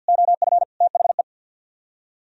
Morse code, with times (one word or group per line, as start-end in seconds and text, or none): 0.1-0.6 OF
0.8-1.2 THE